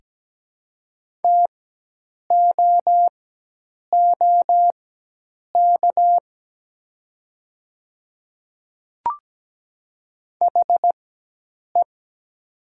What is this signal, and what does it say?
Morse code, heard as a keyed tone, about 17 wpm, spaced as for 8 wpm, 700 Hz. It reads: TOOK HE
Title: 4x Character Spacing - Top 300 Words - 17wpm